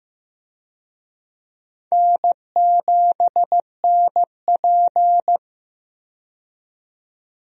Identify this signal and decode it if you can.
Morse code, heard as a keyed tone, about 15 wpm, 700 Hz. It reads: N7NP
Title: USA Call Signs - 4 Characters - 15wpm